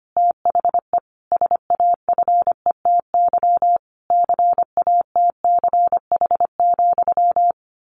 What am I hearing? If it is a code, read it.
Morse code, heard as a keyed tone, about 25 wpm, 700 Hz. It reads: THE SAFETY CATCH,
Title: A Sound Of Thunder 25wpm